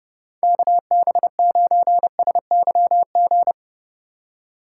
Morse code, heard as a keyed tone, about 30 wpm, 700 Hz. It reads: KB9SYG
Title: USA Call Signs - 6 Characters - 2x Character Spacing - 30wpm